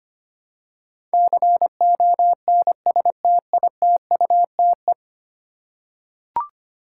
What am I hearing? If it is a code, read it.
Morse code, heard as a keyed tone, about 25 wpm, 700 Hz. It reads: CONSTITUTE
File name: Ten Letter Words - 25wpm